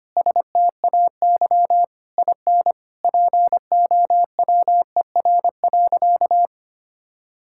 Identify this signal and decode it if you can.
Morse code, heard as a keyed tone, about 25 wpm, 700 Hz. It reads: STAY IN POWER.
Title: Sentences from Top 1000 Words - Review Part 5 - 25wpm